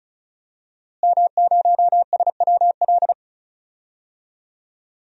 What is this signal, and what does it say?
Morse code, heard as a keyed tone, about 36 wpm, 700 Hz. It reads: M0SWL
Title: DXCC-100 Call Signs - 5 Characters - Repeated 3x - 35wpm